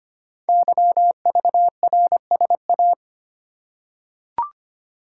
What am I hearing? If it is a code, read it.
Morse code, heard as a keyed tone, about 25 wpm, 700 Hz. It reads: YVRSA
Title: Code Groups - 5 Letters - 25wpm